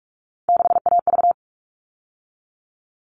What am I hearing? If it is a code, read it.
Morse code, heard as a keyed tone, about 44 wpm, 700 Hz. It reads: 6AV
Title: All Cut Numbers - 3-Digit 45wpm